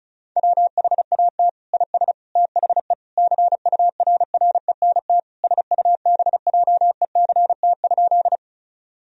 Morse code, heard as a keyed tone, about 34 wpm, 700 Hz. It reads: WHAT IS THE CURRENT SUBJECT?
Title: Sentences from Top 1000 Words - Review Part 4 - 35wpm